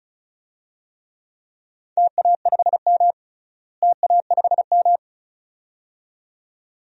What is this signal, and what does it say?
Morse code, heard as a keyed tone, about 36 wpm, 700 Hz. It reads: TA5M TA5M
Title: DXCC-100 Call Signs - 4 Characters - Repeated 2x - 35wpm